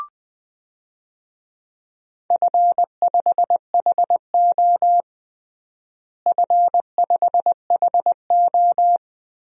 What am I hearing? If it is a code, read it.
Morse code, heard as a keyed tone, about 20 wpm, 700 Hz. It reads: F5HO F5HO
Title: DXCC-100 Call Signs - 4 Characters - Repeated 3x - 20wpm